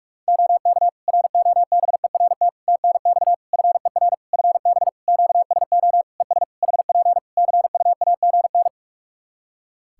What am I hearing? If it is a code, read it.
Morse code, heard as a keyed tone, about 44 wpm, 700 Hz. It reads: OK ROBERT TNX FER FB QSO ES HP CUAGN
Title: Practice QSO - Exchange 4 - Ending - 45wpm